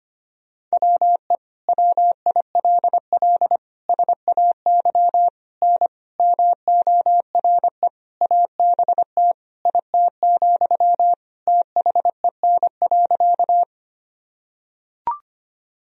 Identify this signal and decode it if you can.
Morse code, heard as a keyed tone, about 25 wpm, 700 Hz. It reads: WE WILL SAY N MORE ABT IT, THEN.
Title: Sentences from Top 200 Words and Abbreviations 25wpm